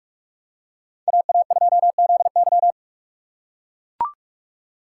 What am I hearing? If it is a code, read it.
Morse code, heard as a keyed tone, about 44 wpm, 700 Hz. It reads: AA2ZY
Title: USA Call Signs - 3x Character Spacing - 45wpm